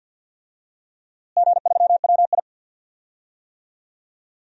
Morse code, heard as a keyed tone, about 47 wpm, 700 Hz. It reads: G3WI
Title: DXCC-100 Call Signs - 4 Characters - Repeated 2x - 50wpm